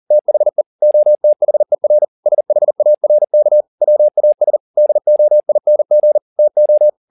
Morse code, heard as a keyed tone, about 40 wpm, 600 Hz. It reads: THE OTHER SHARK WAS DOING TO
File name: the old man and the sea 2 40wpm